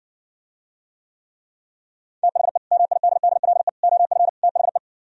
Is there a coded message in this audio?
Yes: Morse code from 2.2 to 4.8 s, heard as a keyed tone, about 70 wpm, 700 Hz. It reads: THE MIDDLE OF THE